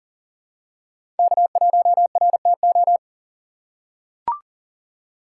Morse code, heard as a keyed tone, about 40 wpm, 700 Hz. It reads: K1RTO
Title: USA Call Signs - 5 Characters - 6x Character Spacing - 40wpm